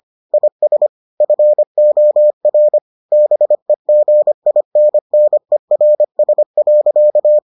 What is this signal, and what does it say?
Morse code, heard as a keyed tone, about 25 wpm, 600 Hz. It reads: IS FOR BEGINNERS.